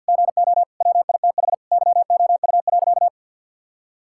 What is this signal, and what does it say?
Morse code, heard as a keyed tone, about 50 wpm, 700 Hz. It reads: GO WITH YOU.
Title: Sentences from Top 1600 Words 50wpm